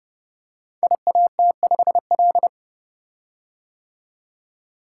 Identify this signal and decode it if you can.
Morse code, heard as a keyed tone, about 30 wpm, 700 Hz. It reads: IAT5L
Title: Code Groups - 5 Letter:Numbers - 30wpm